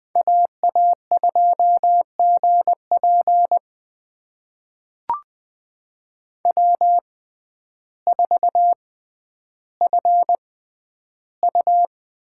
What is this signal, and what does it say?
Morse code, heard as a keyed tone, about 20 wpm, spaced as for 11 wpm, 700 Hz. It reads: AA2GP W4FU
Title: USA Call Signs - 5 Characters - 6x Character Spacing - 20wpm